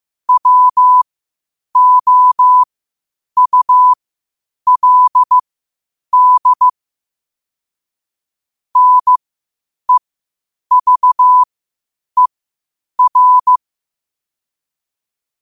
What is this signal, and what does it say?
Morse code, heard as a keyed tone, about 15 wpm, spaced as for 8 wpm, 1000 Hz. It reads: WOULD NEVER
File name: qotdfiles / 8wpm